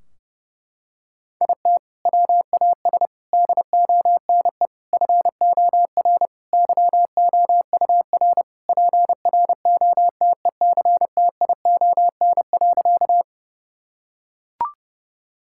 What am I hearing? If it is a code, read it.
Morse code, heard as a keyed tone, about 30 wpm, 700 Hz. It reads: IT WAS DONE FOR YOUR PROTECTION.
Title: Sentences from Top 1600 Words 30wpm